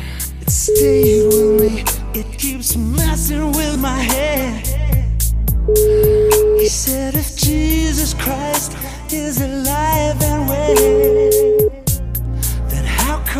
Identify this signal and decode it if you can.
Ringback tone (Europe, 425 Hz), with speech.